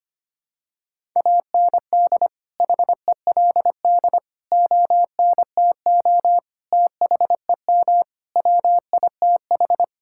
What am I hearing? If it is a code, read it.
Morse code, heard as a keyed tone, about 25 wpm, 700 Hz. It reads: AND HELD ONTO THEM WITH
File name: Sets of 9 Words - 25wpm